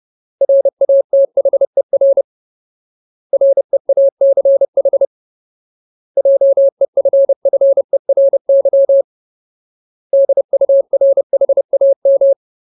Morse code, heard as a keyed tone, about 30 wpm, 550 Hz. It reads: RATHER REACH JEFFERY DURHAM
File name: Ultimate Word List / 30WPM Ultimate